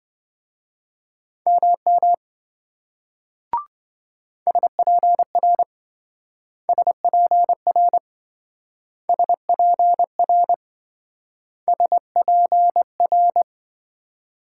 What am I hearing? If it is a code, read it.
Morse code, heard as a keyed tone, about 26 wpm, 700 Hz. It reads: MM SPR SPR SPR SPR